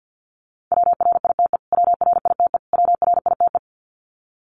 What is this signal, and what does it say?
Morse code, heard as a keyed tone, about 50 wpm, 700 Hz. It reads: WRITE WRITE WRITE
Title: Top 500 Words Repeated 50wpm